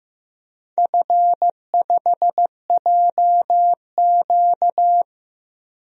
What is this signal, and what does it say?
Morse code, heard as a keyed tone, about 15 wpm, 700 Hz. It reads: F5JQ
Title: DXCC-100 Call Signs - 4 Characters - 15wpm